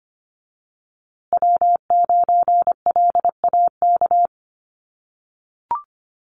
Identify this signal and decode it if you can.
Morse code, heard as a keyed tone, about 25 wpm, 700 Hz. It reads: W9LAK
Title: USA Call Signs Repeated 3x 25wpm